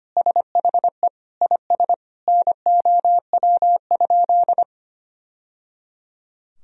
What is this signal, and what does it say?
Morse code, heard as a keyed tone, about 25 wpm, 700 Hz. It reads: SHE IS NOW?
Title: Sentences from Top 1000 Words - Review Part 2 - 25wpm